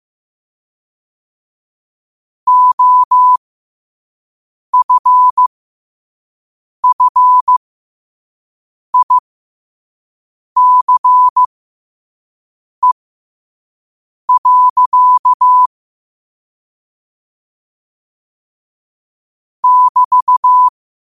Morse code, heard as a keyed tone, about 15 wpm, spaced as for 5 wpm, 1000 Hz. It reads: OFFICE. <BT>
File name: qotdfiles / 5wpm